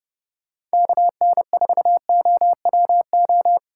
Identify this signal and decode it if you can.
Morse code, heard as a keyed tone, about 30 wpm, 700 Hz. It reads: KN4OWO
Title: Speed Racing - USA 2x3 Call Signs 25wpm